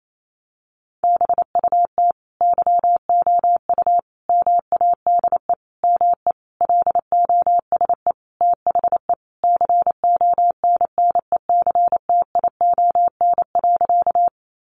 Morse code, heard as a keyed tone, about 28 wpm, 700 Hz. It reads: BUT YOU MADE ME LOSE THE CONNECTION.
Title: Sentences from Top 1300 Words 28wpm